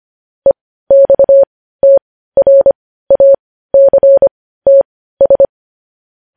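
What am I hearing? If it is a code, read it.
Morse code, heard as a keyed tone, about 25 wpm, spaced as for 15 wpm, 550 Hz. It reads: EXTRACTS
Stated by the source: Morse news headlines practice at 15 wpm.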